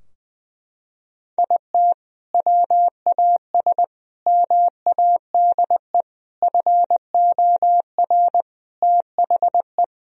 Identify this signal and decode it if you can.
Morse code, heard as a keyed tone, about 20 wpm, 700 Hz. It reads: IT WAS MADE FOR THE
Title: Sentences from Top 1000 Words - Review Part 4 - 20wpm